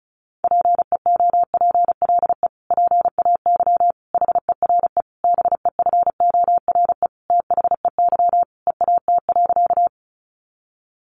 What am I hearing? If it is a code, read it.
Morse code, heard as a keyed tone, about 34 wpm, 700 Hz. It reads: PEOPLE PAY HERE BEFORE THEY EAT.
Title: Sentences from Top 700 Words 35wpm